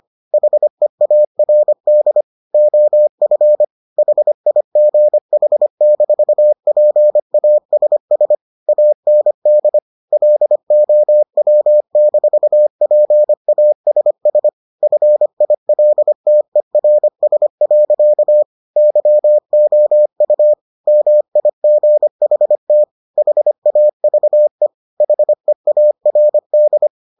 Morse code, heard as a keyed tone, about 25 wpm, 600 Hz. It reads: HEARD OF HIGH-PASS AND LOW-PASS FILTERS. YOU MIGHT HAVE HEARD